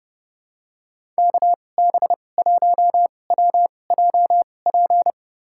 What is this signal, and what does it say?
Morse code, heard as a keyed tone, about 32 wpm, spaced as for 22 wpm, 700 Hz. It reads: KB1WJP
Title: USA Call Signs - 6 Characters - 2x Character Spacing - 30wpm